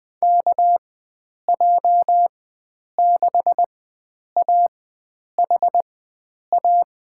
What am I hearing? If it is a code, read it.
Morse code, heard as a keyed tone, about 20 wpm, spaced as for 9 wpm, 700 Hz. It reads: KJ6AHA
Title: USA Call Signs - 6 Characters - 4x Character Spacing - 20wpm